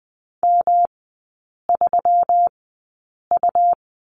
Morse code, heard as a keyed tone, about 20 wpm, spaced as for 8 wpm, 690 Hz. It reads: M3U
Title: 133 Lesson 22e - Question Mark - Three Characters 20wpm